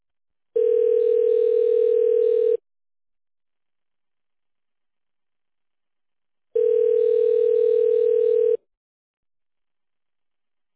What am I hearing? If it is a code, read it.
Ringback tone (North America), 440 + 480 Hz.